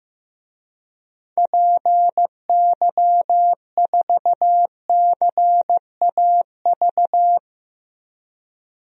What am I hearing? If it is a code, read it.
Morse code, heard as a keyed tone, about 15 wpm, 700 Hz. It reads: PY4CAV
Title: DXCC-100 Call Signs - 6 Characters - 15wpm